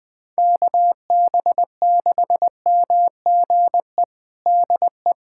Morse code, heard as a keyed tone, about 20 wpm, 700 Hz. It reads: KB6MGE DE
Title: USA Callsign DE Callsign 20wpm